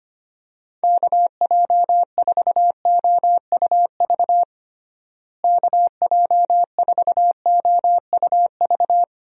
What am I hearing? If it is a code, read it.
Morse code, heard as a keyed tone, about 25 wpm, 700 Hz. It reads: KJ4OUV KJ4OUV
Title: USA Call Signs - 6 Characters - Repeated 3x - 25wpm